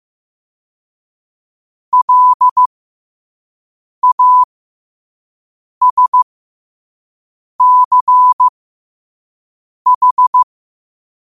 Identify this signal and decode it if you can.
Morse code, heard as a keyed tone, about 15 wpm, spaced as for 5 wpm, 1000 Hz. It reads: LASCH
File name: qotdfiles / 5wpm